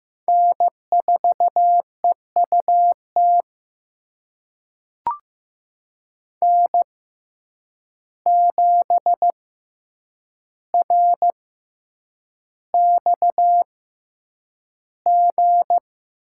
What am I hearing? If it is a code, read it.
Morse code, heard as a keyed tone, about 15 wpm, spaced as for 8 wpm, 700 Hz. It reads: N4EUT N7RXG
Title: USA Call Signs - 5 Characters - 6x Character Spacing - 15wpm